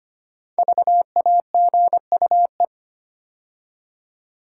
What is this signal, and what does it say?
Morse code, heard as a keyed tone, about 25 wpm, 700 Hz. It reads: VAGUE
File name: 5 Letter Words - First 3 Letters Given - 25wpm